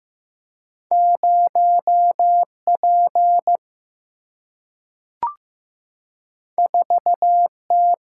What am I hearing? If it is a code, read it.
Morse code, heard as a keyed tone, about 15 wpm, 700 Hz. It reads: 0 P 4 T
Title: Single-Letter Single-Number 15wpm